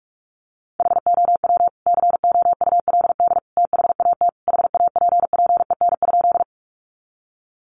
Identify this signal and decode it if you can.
Morse code, heard as a keyed tone, about 44 wpm, 700 Hz. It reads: HOW COULD THAT HAPPEN?
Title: Sentences from Top 400 Words 45wpm